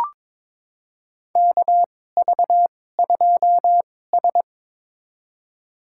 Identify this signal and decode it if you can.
Morse code, heard as a keyed tone, about 22 wpm, spaced as for 16 wpm, 700 Hz. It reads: KV2S
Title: USA Call Signs - 2x Character Spacing - 22wpm